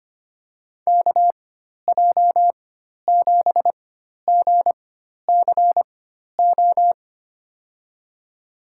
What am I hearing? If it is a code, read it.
Morse code, heard as a keyed tone, about 25 wpm, spaced as for 12 wpm, 700 Hz. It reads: KJ7GCO
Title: USA Call Signs - 4x Character Spacing - 25wpm